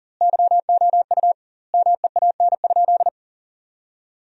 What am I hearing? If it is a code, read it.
Morse code, heard as a keyed tone, about 38 wpm, 700 Hz. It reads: YOU MEAN?
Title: 2x Word-Spacing - Sentences from Top 500 Words - 40wpm